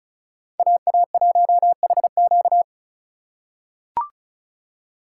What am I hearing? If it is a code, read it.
Morse code, heard as a keyed tone, about 36 wpm, 700 Hz. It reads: AA1HQ
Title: USA Call Signs - 5 Characters - 3x Character Spacing - 35wpm